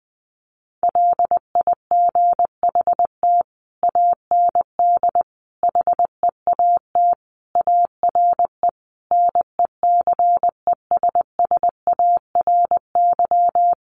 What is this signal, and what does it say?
Morse code, heard as a keyed tone, about 20 wpm, 700 Hz. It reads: LIGHT AND HEAT ARE NECESSARY
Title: Sentences from Top 1300 Words 20wpm